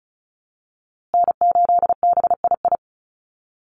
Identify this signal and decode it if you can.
Morse code, heard as a keyed tone, about 34 wpm, 700 Hz. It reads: N8BII
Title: USA Call Signs Repeated 2x 35wpm